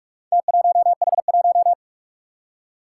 Morse code, heard as a keyed tone, about 46 wpm, 700 Hz. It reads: T1H1
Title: Four Challenging Letter-Number Combinations 45wpm